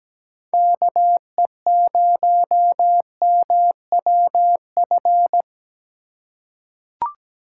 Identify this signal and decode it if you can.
Morse code, heard as a keyed tone, about 17 wpm, 700 Hz. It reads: KE0MWF
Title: USA Call Signs - 3x Character Spacing - 17wpm